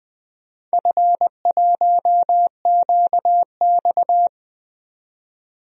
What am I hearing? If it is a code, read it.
Morse code, heard as a keyed tone, about 20 wpm, 700 Hz. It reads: F1QX
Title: DXCC-100 Call Signs - 4 Characters - 20wpm